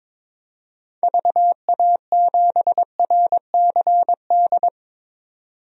Morse code, heard as a keyed tone, about 22 wpm, 700 Hz. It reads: VA7RCD
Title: DXCC-100 Call Signs - 6 Characters - 22wpm